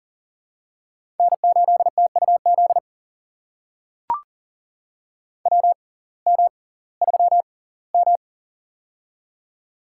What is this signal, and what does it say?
Morse code, heard as a keyed tone, about 40 wpm, spaced as for 22 wpm, 700 Hz. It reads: N8TUZ WM3M
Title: USA Call Signs - 6x Character Spacing - 40wpm